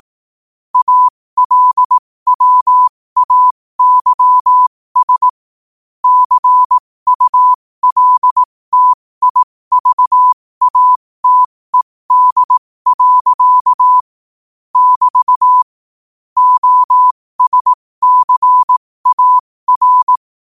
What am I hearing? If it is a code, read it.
Morse code, heard as a keyed tone, about 18 wpm, spaced as for 15 wpm, 1000 Hz. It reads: ALWAYS CULTIVATED. <BT> OSCAR
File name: qotdfiles / 15wpm